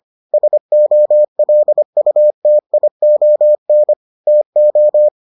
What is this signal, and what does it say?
Morse code, heard as a keyed tone, about 25 wpm, 600 Hz. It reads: SOLUTION TO